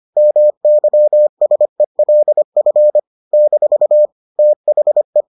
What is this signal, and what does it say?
Morse code, heard as a keyed tone, about 25 wpm, 600 Hz. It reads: MYSELF - THE